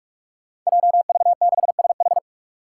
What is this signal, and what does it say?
Morse code, heard as a keyed tone, about 44 wpm, 700 Hz. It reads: JV6SH
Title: Five Challenging Letter-Number Combinations 45wpm